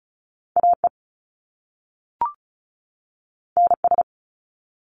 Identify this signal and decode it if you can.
Morse code, heard as a keyed tone, about 36 wpm, 700 Hz. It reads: AE NS
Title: Top Two Letter Combinations in Words 35wpm